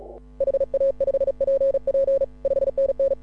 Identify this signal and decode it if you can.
Morse code, heard as a keyed tone, about 36 wpm, 530 Hz. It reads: HA5PP 5NN